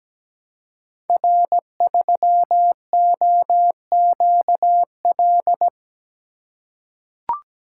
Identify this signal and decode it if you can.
Morse code, heard as a keyed tone, about 17 wpm, 700 Hz. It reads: R3OQL
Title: Code Groups - 5 Letter:Numbers - 17wpm